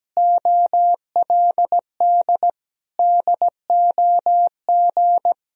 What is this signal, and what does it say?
Morse code, heard as a keyed tone, about 17 wpm, 700 Hz. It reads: OLD DOG
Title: Sentences from Top 1000 Words - Review Part 2 - 17wpm